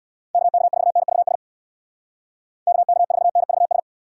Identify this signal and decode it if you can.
Morse code, heard as a keyed tone, about 64 wpm, 700 Hz. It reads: KK4NVS KK4NVS